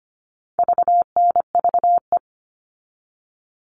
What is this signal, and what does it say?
Morse code, heard as a keyed tone, about 25 wpm, 700 Hz. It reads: VNVE
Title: All Cut Numbers - 4-Digit 25wpm